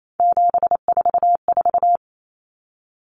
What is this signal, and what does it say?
Morse code, heard as a keyed tone, about 28 wpm, 700 Hz. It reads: 744
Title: Numbers - 3-Digit 28wpm